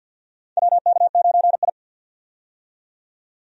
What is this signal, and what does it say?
Morse code, heard as a keyed tone, about 50 wpm, 700 Hz. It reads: WK9I
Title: USA Call Signs - 4 Characters - 50wpm